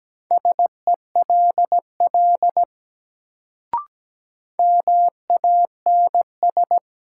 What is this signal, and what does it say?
Morse code, heard as a keyed tone, about 17 wpm, 700 Hz. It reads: SELL MANS
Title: Sentences from Top 1000 Words - Review Part 6 - New Word Drill - 17wpm